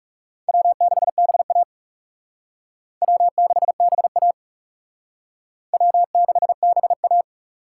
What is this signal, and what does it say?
Morse code, heard as a keyed tone, about 37 wpm, 700 Hz. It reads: W6BA W6BA W6BA
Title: Speed Racing - USA Call Signs 30wpm